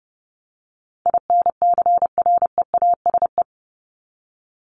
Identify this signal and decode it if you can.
Morse code, heard as a keyed tone, about 30 wpm, 700 Hz. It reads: INCREASE
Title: LY Suffix - Top 100 Words with Base Words 30wpm